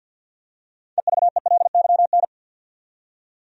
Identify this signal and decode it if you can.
Morse code, heard as a keyed tone, about 47 wpm, 700 Hz. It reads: EVELYN
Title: USA Common Names - 50wpm